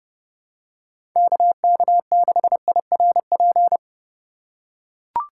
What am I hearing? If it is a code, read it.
Morse code, heard as a keyed tone, about 29 wpm, 700 Hz. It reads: KK6IRP
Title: USA Call Signs - 6 Characters - 3x Character Spacing - 30wpm